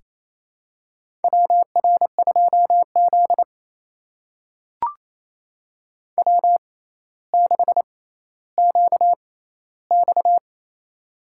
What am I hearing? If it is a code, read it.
Morse code, heard as a keyed tone, about 28 wpm, spaced as for 14 wpm, 700 Hz. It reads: WR2Z W6QX
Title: USA Call Signs - 4 Characters - 6x Character Spacing - 28wpm